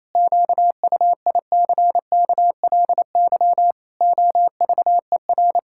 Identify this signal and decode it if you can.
Morse code, heard as a keyed tone, about 28 wpm, 700 Hz. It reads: QUICKLY OVER
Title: Pangram Challenge v2 - 28wpm